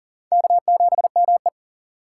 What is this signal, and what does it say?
Morse code, heard as a keyed tone, about 38 wpm, 700 Hz. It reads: K7ME
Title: USA Call Signs - 4 Characters - 4x Character Spacing - 40wpm